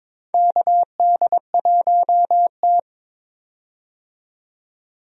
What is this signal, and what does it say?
Morse code, heard as a keyed tone, about 22 wpm, 700 Hz. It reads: KD1T
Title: USA Call Signs - 4 Characters - 22wpm